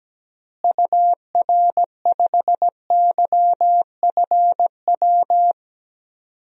Morse code, heard as a keyed tone, about 17 wpm, 700 Hz. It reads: UR5YFW
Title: DXCC-100 Call Signs - 6 Characters - Repeated 2x - 17wpm